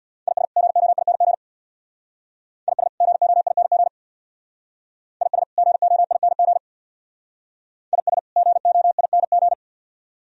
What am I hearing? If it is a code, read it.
Morse code, heard as a keyed tone, about 62 wpm, 690 Hz. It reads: IS GOING IS GOING IS GOING IS GOING
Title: Speed-Racing - Sets of 2 Words 50wpm